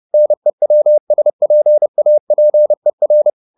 Morse code, heard as a keyed tone, about 30 wpm, 600 Hz. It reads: NEWSPAPER
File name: the old man and the sea / the old man and the sea 1 30wpm